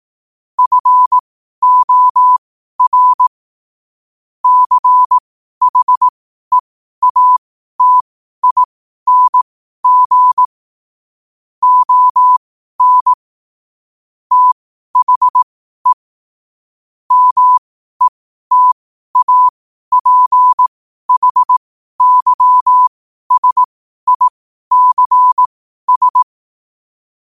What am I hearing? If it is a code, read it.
Morse code, heard as a keyed tone, about 18 wpm, spaced as for 12 wpm, 1000 Hz. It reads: FOR CHEATING ON THE METAPHYSICS
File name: qotdfiles / 12wpm